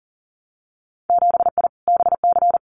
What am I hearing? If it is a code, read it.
Morse code, heard as a keyed tone, about 39 wpm, 700 Hz. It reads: 7I BC
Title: Winter Field Day Exchanges Encore 40wpm